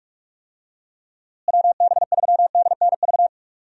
Morse code, heard as a keyed tone, about 42 wpm, 700 Hz. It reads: WB3DNV
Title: USA Call Signs - 6 Characters - Repeated 2x - 45wpm